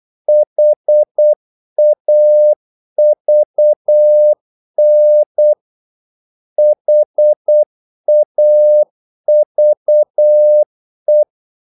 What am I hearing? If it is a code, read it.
Morse code, heard as a keyed tone, about 8 wpm, 600 Hz. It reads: HAVN HAVE
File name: Korte ord DK 8wpm